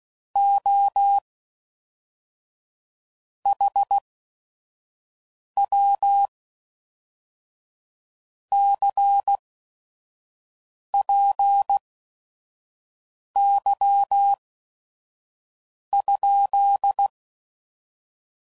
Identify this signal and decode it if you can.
Morse code, heard as a keyed tone, about 16 wpm, spaced as for 5 wpm, 800 Hz. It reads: O HW CPY?
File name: CW Code Groups @5WPM / QSOs / QSO WD0FBT